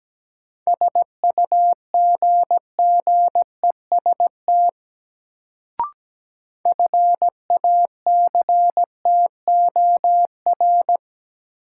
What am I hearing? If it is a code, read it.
Morse code, heard as a keyed tone, about 17 wpm, 700 Hz. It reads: SUGGEST FACTOR
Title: Sentences from Top 1300 Words - New Word Drill - 17wpm